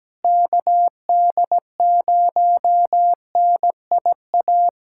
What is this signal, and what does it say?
Morse code, heard as a keyed tone, about 17 wpm, 700 Hz. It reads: KD0NIA
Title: USA Callsign DE Callsign 17wpm